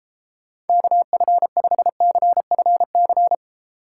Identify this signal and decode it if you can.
Morse code, heard as a keyed tone, about 34 wpm, 700 Hz. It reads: KF5CFC